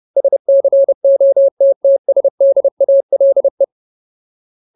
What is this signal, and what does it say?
Morse code, heard as a keyed tone, about 30 wpm, 550 Hz. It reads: SCOTTSDALE
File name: Ultimate Word List / 30WPM Ultimate